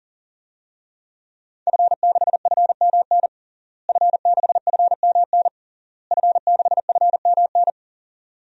Morse code, heard as a keyed tone, about 37 wpm, 700 Hz. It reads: F6FMN F6FMN F6FMN
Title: DXCC-100 Call Signs - 5 Characters - Repeated 3x - 40wpm